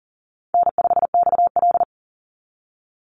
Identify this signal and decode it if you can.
Morse code, heard as a keyed tone, about 40 wpm, 700 Hz. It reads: N5XL
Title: NA SOTA Chaser Callsigns 40wpm